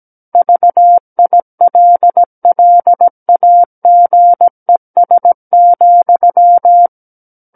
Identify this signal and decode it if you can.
Morse code, heard as a keyed tone, about 17 wpm, 700 Hz. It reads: VILLAGES,
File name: Book One / 08-Chapter 8 - 17 WPM